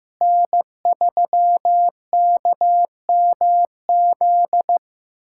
Morse code, heard as a keyed tone, about 15 wpm, 700 Hz. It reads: N3KMZ
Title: USA Callsign DE Callsign - 5 Characters - 15wpm